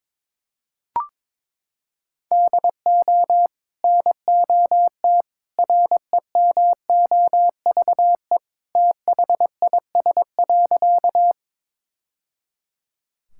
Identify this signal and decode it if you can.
Morse code, heard as a keyed tone, about 22 wpm, 700 Hz. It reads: DO NOT REMOVE THIS.
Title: Sentences from Top 1000 Words - Review Part 5 - 22wpm